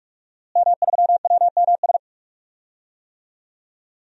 Morse code, heard as a keyed tone, about 43 wpm, 700 Hz. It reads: M3WMS
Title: DXCC-100 Call Signs - 5 Characters - Repeated 3x - 45wpm